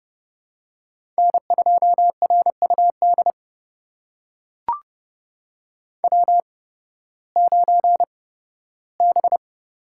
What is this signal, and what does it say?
Morse code, heard as a keyed tone, about 30 wpm, spaced as for 16 wpm, 700 Hz. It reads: N2RUD W9B
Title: USA Call Signs - 5 Characters - 8x Character Spacing - 30wpm